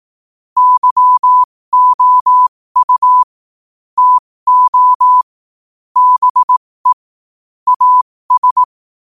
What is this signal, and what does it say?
Morse code, heard as a keyed tone, about 18 wpm, spaced as for 15 wpm, 1000 Hz. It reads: YOU TO BE AS
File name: qotdfiles / 15wpm